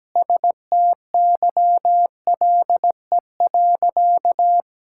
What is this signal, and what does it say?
Morse code, heard as a keyed tone, about 17 wpm, 700 Hz. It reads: STYLE.
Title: Sentences from Top 1000 Words - Review Part 5 - 17wpm